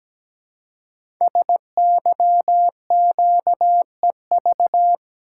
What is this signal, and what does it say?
Morse code, heard as a keyed tone, about 17 wpm, 700 Hz. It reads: SYQEV
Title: Code Groups - 5 Letters - 17wpm